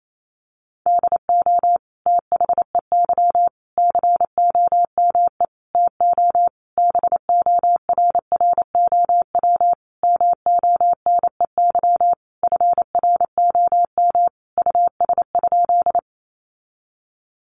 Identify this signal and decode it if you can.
Morse code, heard as a keyed tone, about 28 wpm, 700 Hz. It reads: DO THEY COME TO BORROW MONEY FROM US?
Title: Sentences from Top 1500 Words 28wpm